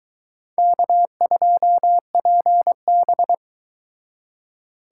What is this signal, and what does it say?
Morse code, heard as a keyed tone, about 23 wpm, 700 Hz. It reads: K2PB